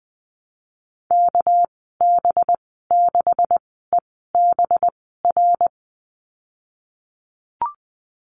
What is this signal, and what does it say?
Morse code, heard as a keyed tone, about 20 wpm, spaced as for 15 wpm, 700 Hz. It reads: KB6EBR
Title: 213 Lesson 34g - Number 8 - Call Signs 20wpm